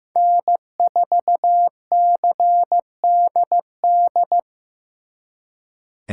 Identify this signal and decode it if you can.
Morse code, heard as a keyed tone, about 18 wpm, 700 Hz. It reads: N4CDD
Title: Speed Racing - USA 1x3 Call Signs 15wpm